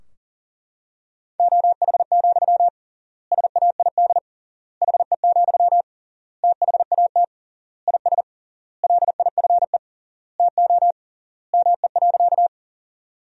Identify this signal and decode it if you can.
Morse code, heard as a keyed tone, about 37 wpm, 700 Hz. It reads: OH, SAID HE, THAT IS LIFE TO ME.
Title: 3x Word-Spacing - Sentences from Top 500 Words - 40wpm